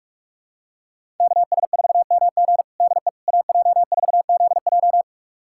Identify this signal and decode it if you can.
Morse code, heard as a keyed tone, about 46 wpm, 700 Hz. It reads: KS4MG DE AJ4ZJ
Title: USA Callsign DE Callsign - 5 Characters - 45wpm